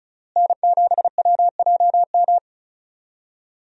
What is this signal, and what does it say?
Morse code, heard as a keyed tone, about 34 wpm, 700 Hz. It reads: N7WJM
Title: USA Call Signs - 5 Characters - Repeated 3x - 35wpm